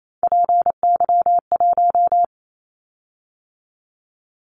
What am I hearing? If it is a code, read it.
Morse code, heard as a keyed tone, about 28 wpm, 700 Hz. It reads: PY1
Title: Callsign Prefixes for Top 50 DXCC - 28wpm